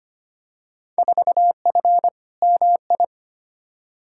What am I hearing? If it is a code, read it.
Morse code, heard as a keyed tone, about 25 wpm, 700 Hz. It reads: 4F MI
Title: Field Day Exchanges Encore - 25wpm